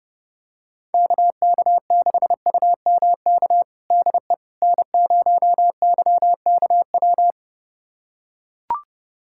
Morse code, heard as a keyed tone, about 30 wpm, 700 Hz. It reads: KK6UMK DE N0YKW
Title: USA Callsign DE Callsign 30wpm